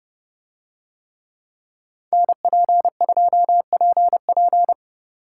Morse code, heard as a keyed tone, about 29 wpm, 700 Hz. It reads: NP2PP